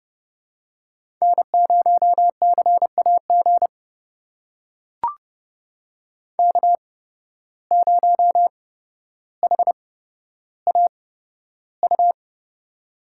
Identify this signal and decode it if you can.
Morse code, heard as a keyed tone, about 32 wpm, spaced as for 13 wpm, 700 Hz. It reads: N0CAG K0HAU
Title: USA Call Signs - 5 Characters - 8x Character Spacing - 30wpm